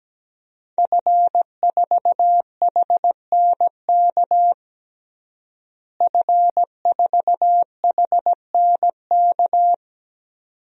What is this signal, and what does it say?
Morse code, heard as a keyed tone, about 17 wpm, 700 Hz. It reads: F4HNK F4HNK
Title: DXCC-100 Call Signs - 5 Characters - Repeated 2x - 17wpm